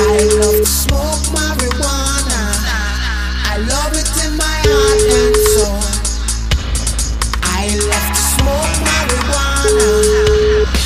Ringback tone (Europe, 425 Hz), with noise.